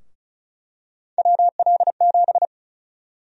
Morse code, heard as a keyed tone, about 36 wpm, 700 Hz. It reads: WL7
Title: USA Call Sign Prefixes - 35wpm